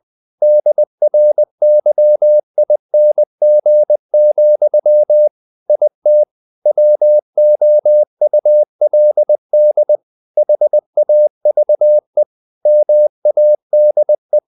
Morse code, heard as a keyed tone, about 20 wpm, 600 Hz. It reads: DRYING, IT WOULD HAVE MADE
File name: the old man and the sea part 2 20wpm